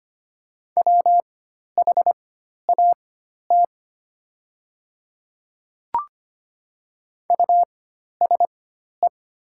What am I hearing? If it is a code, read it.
Morse code, heard as a keyed tone, about 25 wpm, spaced as for 12 wpm, 690 Hz. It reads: WHAT USE
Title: 4x Character Spacing - Top 100 Words - 25wpm